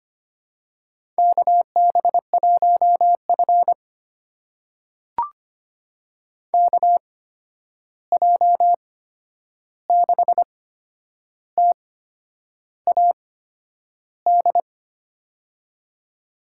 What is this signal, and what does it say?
Morse code, heard as a keyed tone, about 25 wpm, spaced as for 9 wpm, 700 Hz. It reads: KB1F KJ6TAD
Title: USA Call Signs - 8x Character Spacing - 25wpm